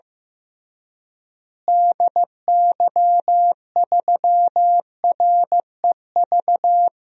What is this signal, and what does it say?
Morse code, heard as a keyed tone, about 15 wpm, 700 Hz. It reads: DY3REV
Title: DXCC-100 Call Signs - 6 Characters - Repeated 3x - 15wpm